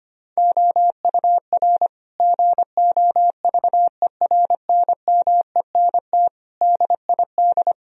Morse code, heard as a keyed tone, about 25 wpm, 700 Hz. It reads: OUR GOVERNMENT DID
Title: Sentences from Top 1000 Words - Review Part 3 - 25wpm